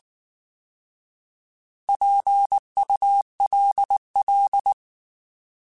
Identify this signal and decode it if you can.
Morse code, heard as a keyed tone, about 19 wpm, 800 Hz. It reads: PULL
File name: Alphabetical / Alphabetical mp3s / Ps @18